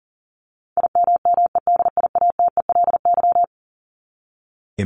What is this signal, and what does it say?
Morse code, heard as a keyed tone, about 40 wpm, 700 Hz. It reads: IMMEDIATELY
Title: LY Suffix - Top 100 Words with Base Words 40wpm